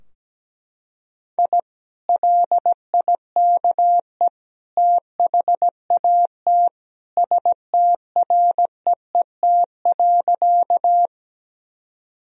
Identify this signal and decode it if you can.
Morse code, heard as a keyed tone, about 17 wpm, 700 Hz. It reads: I LIKE THAT STREET.
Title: Sentences from Top 1000 Words - Review Part 3 - 17wpm